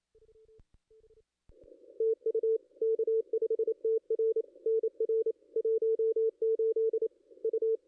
Morse code, heard as a keyed tone, about 26 wpm, 450 Hz. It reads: TU K5TR NR 18 U
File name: contests / ss-cw-k5tr-op-n5rz -2200